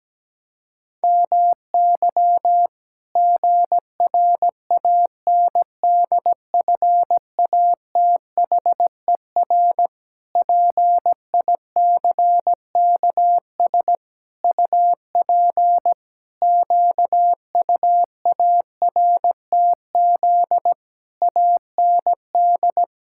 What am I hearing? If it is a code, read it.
Morse code, heard as a keyed tone, about 17 wpm, 700 Hz. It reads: MY GRANDFATHER PICKS UP QUARTZ AND
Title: Pangram Challenge v2 - 17wpm